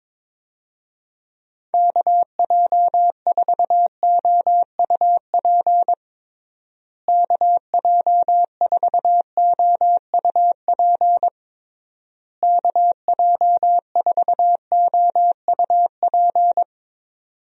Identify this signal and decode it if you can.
Morse code, heard as a keyed tone, about 22 wpm, 700 Hz. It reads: KJ4OUP KJ4OUP KJ4OUP
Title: USA Call Signs - 6 Characters - Repeated 3x - 22wpm